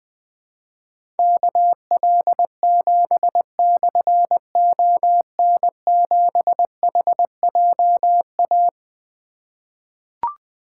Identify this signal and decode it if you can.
Morse code, heard as a keyed tone, about 21 wpm, 700 Hz. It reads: KL7/ON7HJA
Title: Difficult Call Signs 20wpm